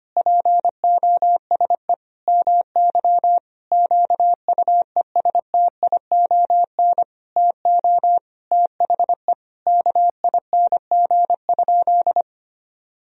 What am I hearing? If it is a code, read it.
Morse code, heard as a keyed tone, about 25 wpm, 700 Hz. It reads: POSE MY QUESTION TO THE KING?
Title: Sentences from Top 1000 Words - Review Part 2 - 25wpm